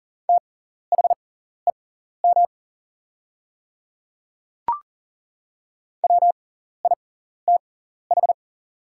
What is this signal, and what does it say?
Morse code, heard as a keyed tone, about 40 wpm, spaced as for 14 wpm, 700 Hz. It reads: THEM WITH
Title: 6x Character Spacing - Top 100 Words - 40wpm